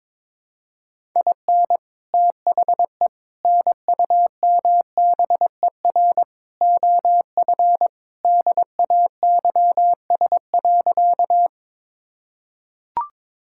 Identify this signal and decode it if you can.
Morse code, heard as a keyed tone, about 22 wpm, 700 Hz. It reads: IN THE NUMBER OF DAYS.
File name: practice / 22wpm / Sentences with 4-Word Phrases - 22wpm